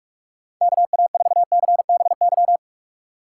Morse code, heard as a keyed tone, about 46 wpm, 700 Hz. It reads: KA4CBY
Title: USA Call Signs - 6 Characters - 3x Character Spacing - 45wpm